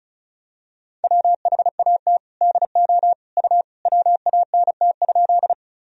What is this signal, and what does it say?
Morse code, heard as a keyed tone, about 36 wpm, 700 Hz. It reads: WHAT DO U WANT?
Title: Sentences from Top 200 Words and Abbreviations 35wpm